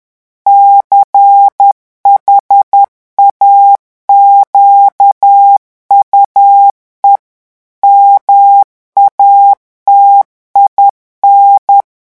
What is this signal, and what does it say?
Morse code, heard as a keyed tone, about 10 wpm, 800 Hz. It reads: CHAQUE MATIN
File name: morse mp3 / revision francais 10 mots minute